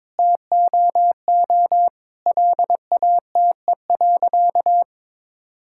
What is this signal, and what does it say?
Morse code, heard as a keyed tone, about 22 wpm, 700 Hz. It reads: TOO LATE.
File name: Sentences with 7-Word Phrases - 22wpm